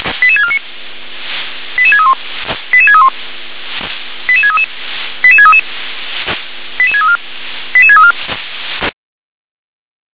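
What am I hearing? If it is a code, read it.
Five-tone selective call (ZVEI), heard as quick runs of five tones, tones of about 70 ms. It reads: 88649 88631 88631 88633 88633 88634 88634
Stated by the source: Five-tone ZVEI selective call received on SDR.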